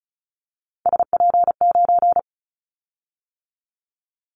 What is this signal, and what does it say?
Morse code, heard as a keyed tone, about 36 wpm, 700 Hz. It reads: SP9
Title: Callsign Prefixes for Top 25 DXCC - 35wpm